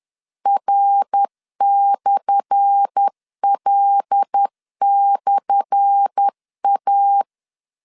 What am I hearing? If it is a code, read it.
Morse code, heard as a keyed tone, about 10 wpm, 800 Hz. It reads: R/L/A
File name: LEZIONI / 10 SIGNS / 10A SLASH 10wpm 2808